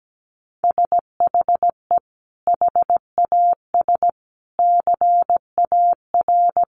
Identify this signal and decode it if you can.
Morse code, heard as a keyed tone, about 17 wpm, 700 Hz. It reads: SHE HAS CAR
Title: Sentences from Top 1100 Words 17wpm